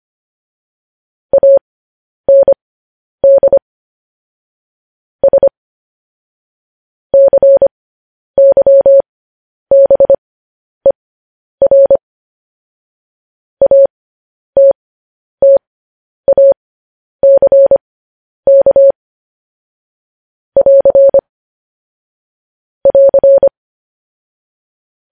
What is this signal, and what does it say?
Morse code, heard as a keyed tone, about 25 wpm, spaced as for 10 wpm, 550 Hz. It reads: AND S CYBER ATTACK <AR> <AR>
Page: News Headlines in Morse Code at 10 WPM